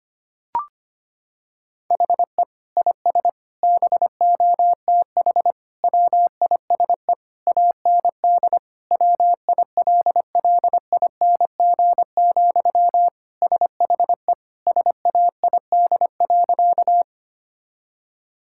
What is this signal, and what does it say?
Morse code, heard as a keyed tone, about 25 wpm, 700 Hz. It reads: HE IS BOTH WISE AND WILLING, SHE SAID.
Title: Sentences from Top 1600 Words 25wpm